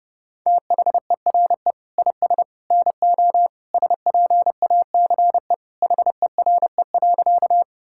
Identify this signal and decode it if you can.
Morse code, heard as a keyed tone, about 30 wpm, 700 Hz. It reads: THERE IS NO SPACE HERE.
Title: Sentences from Top 1000 Words - Review Part 5 - 30wpm